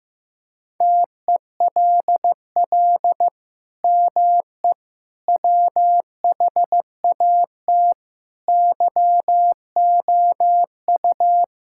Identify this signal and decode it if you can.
Morse code, heard as a keyed tone, about 15 wpm, 700 Hz. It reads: TELL ME WHAT YOU
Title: Sentences from Top 1600 Words 15wpm